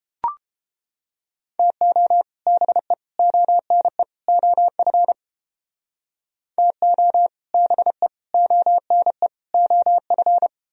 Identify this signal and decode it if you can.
Morse code, heard as a keyed tone, about 30 wpm, 700 Hz. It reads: TO BE ONE OF TO BE ONE OF